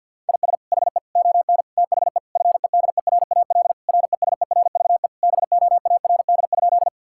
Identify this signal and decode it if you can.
Morse code, heard as a keyed tone, about 50 wpm, 700 Hz. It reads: IS HE ON THE FEDERAL RESERVE BOARD?
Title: Sentences from Top 1000 Words - Review Part 6 - 50wpm